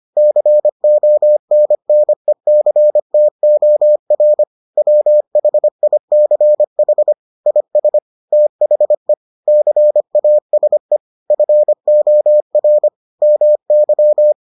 Morse code, heard as a keyed tone, about 25 wpm, 600 Hz. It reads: CONNECTOR WHICH IS THE CASE FOR MY